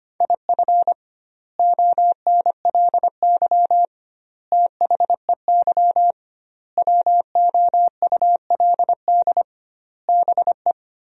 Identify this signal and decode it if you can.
Morse code, heard as a keyed tone, about 25 wpm, 700 Hz. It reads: IF ONLY THEY WOULD BE
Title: 2x Word-Spacing - Sentences from Top 500 Words - 25wpm